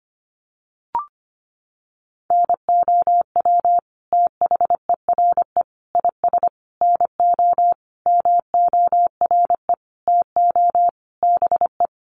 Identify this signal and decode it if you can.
Morse code, heard as a keyed tone, about 25 wpm, 700 Hz. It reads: NOW THERE IS NO MORE TO BE
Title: Sentences from Top 100 Words 25wpm